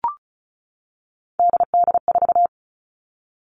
Morse code, heard as a keyed tone, about 36 wpm, 700 Hz. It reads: DD4
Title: Three Challenging Letter-Number Combinations 35wpm